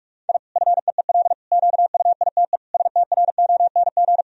Morse code, heard as a keyed tone, about 46 wpm, 700 Hz. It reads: I FEEL QUITE STRONG
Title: Sentences from Top 1000 Words - Review Part 3 - 45wpm